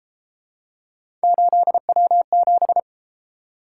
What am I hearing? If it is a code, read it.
Morse code, heard as a keyed tone, about 32 wpm, 700 Hz. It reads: 8W7